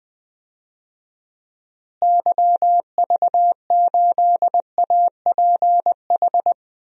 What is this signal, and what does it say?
Morse code, heard as a keyed tone, about 20 wpm, 700 Hz. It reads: YV8APH
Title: DXCC-100 Call Signs - 6 Characters - Repeated 3x - 20wpm